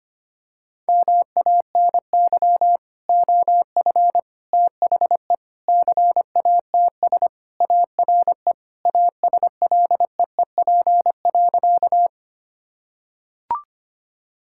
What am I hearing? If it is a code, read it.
Morse code, heard as a keyed tone, about 25 wpm, 700 Hz. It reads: MANY OF THE CATS ARE ASLEEP.
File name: practice / 25wpm / Sentences with 3-Word Phrases - 25wpm